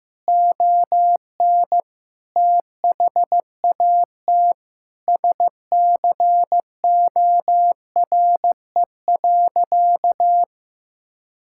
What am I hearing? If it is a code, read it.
Morse code, heard as a keyed tone, about 16 wpm, 700 Hz. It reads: ON THAT SCORE.
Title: Sentences from Top 1600 Words 15wpm